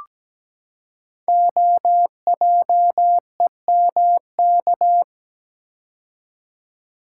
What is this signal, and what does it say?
Morse code, heard as a keyed tone, about 17 wpm, 700 Hz. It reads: OJEMK
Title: Code Groups - 5 Letters - 17wpm